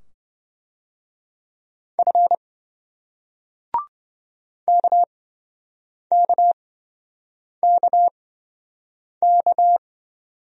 Morse code, heard as a keyed tone, about 26 wpm, spaced as for 7 wpm, 700 Hz. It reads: F KKKK